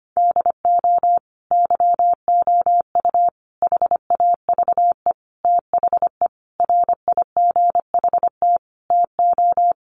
Morse code, heard as a keyed tone, about 25 wpm, 700 Hz. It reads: DO YOU HAVE THE RIGHT TO
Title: Sentences from Top 1400 Words 25wpm